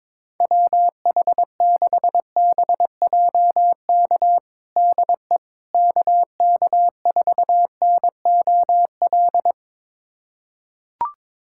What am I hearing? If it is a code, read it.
Morse code, heard as a keyed tone, about 22 wpm, 700 Hz. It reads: WH6BJK DE KK4NOL
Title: USA Callsign DE Callsign 22wpm